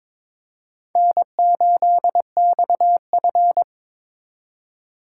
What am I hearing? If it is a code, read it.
Morse code, heard as a keyed tone, about 26 wpm, 700 Hz. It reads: N8XF
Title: Speed Racing - USA Call Signs 22wpm